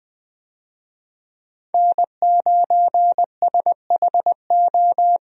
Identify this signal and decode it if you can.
Morse code, heard as a keyed tone, about 20 wpm, 700 Hz. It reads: N9SHO
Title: USA Call Signs - 5 Characters - Repeated 2x - 20wpm